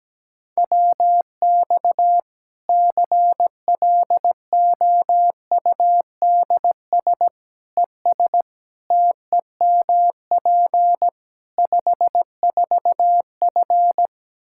Morse code, heard as a keyed tone, about 17 wpm, 700 Hz. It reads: WX CLOUDS ES TEMP 54F
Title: Practice QSO - Exchange 2 - Info - 17wpm